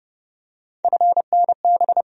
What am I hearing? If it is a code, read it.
Morse code, heard as a keyed tone, about 30 wpm, 700 Hz. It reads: FNB
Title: USA Call Sign Postfixes - 3 Letters 30wpm